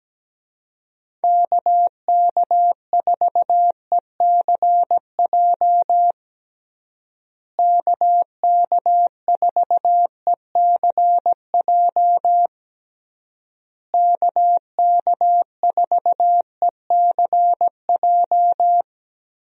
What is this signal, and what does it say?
Morse code, heard as a keyed tone, about 17 wpm, 700 Hz. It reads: KK4ECJ KK4ECJ KK4ECJ
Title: USA Call Signs - 6 Characters - Repeated 3x - 17wpm